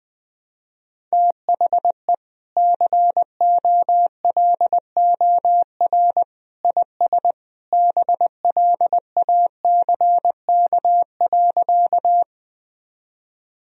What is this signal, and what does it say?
Morse code, heard as a keyed tone, about 20 wpm, 700 Hz. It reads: THE COLOR IS BLACK.
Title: Sentences from Top 1000 Words - Review Part 2 - 20wpm